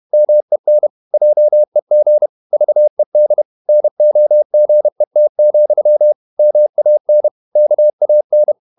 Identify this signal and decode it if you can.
Morse code, heard as a keyed tone, about 31 wpm, 600 Hz. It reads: MEN JEG VED NOGET, MAN KAN
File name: Ferie QRQ 31wpm